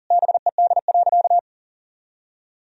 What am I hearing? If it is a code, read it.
Morse code, heard as a keyed tone, about 40 wpm, 700 Hz. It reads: BED.
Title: Sentences from Top 1000 Words - Review Part 4 - 40wpm